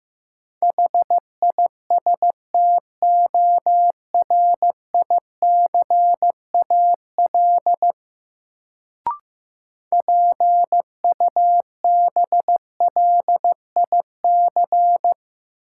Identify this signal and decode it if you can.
Morse code, heard as a keyed tone, about 15 wpm, 700 Hz. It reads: HISTORICAL PUBLIC
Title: Sentences from Top 600 Words - New Word Drill - 15wpm